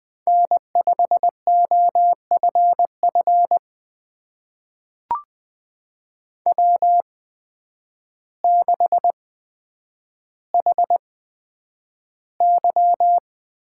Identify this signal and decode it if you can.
Morse code, heard as a keyed tone, about 20 wpm, spaced as for 9 wpm, 700 Hz. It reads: N5OFF W6HY
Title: USA Call Signs - 8x Character Spacing - 20wpm